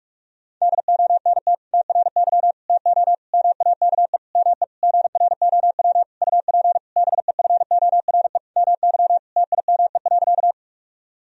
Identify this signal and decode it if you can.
Morse code, heard as a keyed tone, about 46 wpm, 700 Hz. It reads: DONT TRY TO MAKE ME GROW UP BEFORE MY TIME.
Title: Sentences from Top 1000 Words - Review Part 1 - 45wpm